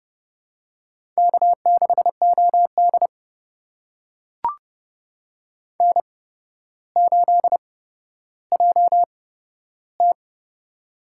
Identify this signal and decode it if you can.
Morse code, heard as a keyed tone, about 30 wpm, spaced as for 13 wpm, 700 Hz. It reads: K6OD N8JT
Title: USA Call Signs - 4 Characters - 8x Character Spacing - 30wpm